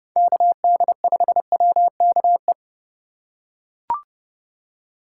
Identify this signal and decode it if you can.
Morse code, heard as a keyed tone, about 30 wpm, 700 Hz. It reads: KD5WKE
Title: USA Call Signs - 3x Character Spacing - 30wpm